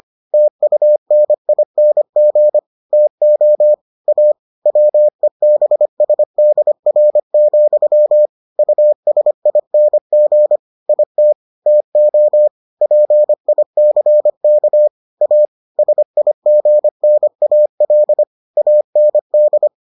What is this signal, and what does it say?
Morse code, heard as a keyed tone, about 25 wpm, 600 Hz. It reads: TUNING TO A WEBSDR, USING IT TO PICK A SIGNAL AND